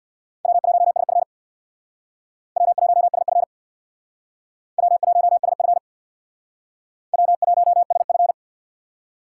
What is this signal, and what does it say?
Morse code, heard as a keyed tone, about 61 wpm, 690 Hz. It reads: W1SF W1SF W1SF W1SF
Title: Speed Racing - USA 1x2 Call Signs 50wpm